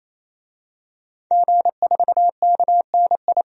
Morse code, heard as a keyed tone, about 28 wpm, 700 Hz. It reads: G4KNI
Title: DXCC-100 Call Signs - 5 Characters - 28wpm